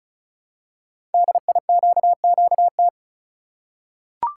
Morse code, heard as a keyed tone, about 36 wpm, 700 Hz. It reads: DIQQT
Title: Code Groups - 5 Letters - 35wpm